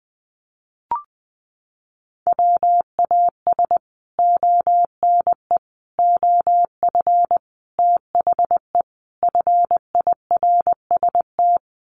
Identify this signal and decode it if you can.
Morse code, heard as a keyed tone, about 19 wpm, 700 Hz. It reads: WAS ONE OF THE FIRST
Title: Sets of 5 Words 20wpm